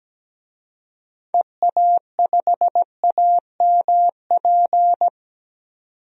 Morse code, heard as a keyed tone, about 17 wpm, 700 Hz. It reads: EA5AMP
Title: DXCC-100 Call Signs - 6 Characters - 17wpm